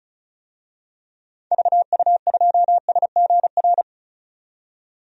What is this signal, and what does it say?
Morse code, heard as a keyed tone, about 34 wpm, 700 Hz. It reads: VU2SGR
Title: DXCC-100 Call Signs - 6 Characters - 35wpm